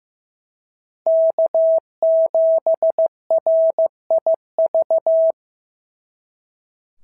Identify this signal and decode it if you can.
Morse code, heard as a keyed tone, about 14 wpm, 650 Hz. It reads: K7RIV
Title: SOTA QSO Practice 15wpm